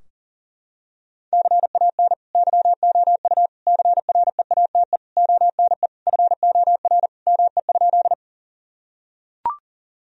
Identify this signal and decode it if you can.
Morse code, heard as a keyed tone, about 40 wpm, 700 Hz. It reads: CAN YOU CREATE ONE FOR ME?
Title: Sentences from Top 1000 Words - Review Part 3 - 40wpm